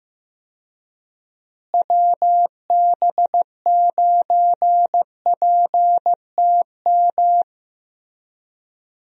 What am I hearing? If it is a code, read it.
Morse code, heard as a keyed tone, about 15 wpm, 700 Hz. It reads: WB9PTM
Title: USA Call Signs - 6 Characters - Repeated 3x - 15wpm